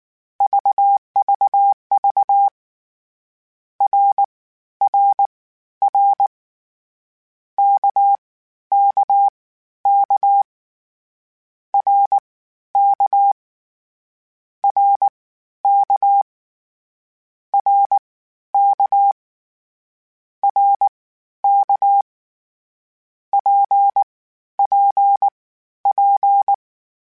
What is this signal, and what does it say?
Morse code, heard as a keyed tone, about 19 wpm, spaced as for 12 wpm, 800 Hz. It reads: VVV RRR KKK RK RK RK RK PPP